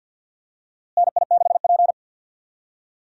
Morse code, heard as a keyed tone, about 51 wpm, 700 Hz. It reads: NI6P
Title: USA Call Signs - 4 Characters - Repeated 2x - 50wpm